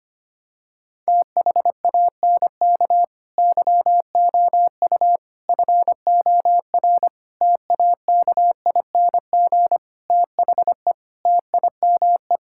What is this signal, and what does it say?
Morse code, heard as a keyed tone, about 25 wpm, 700 Hz. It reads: THANK YOU FOR TAKING THE TIME
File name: practice / 25wpm / Sentences with 9-Word Phrases - 25wpm